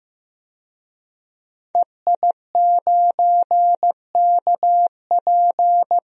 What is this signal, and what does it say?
Morse code, heard as a keyed tone, about 16 wpm, 700 Hz. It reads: EI9KP
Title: DXCC-100 Call Signs - 5 Characters - Repeated 2x - 15wpm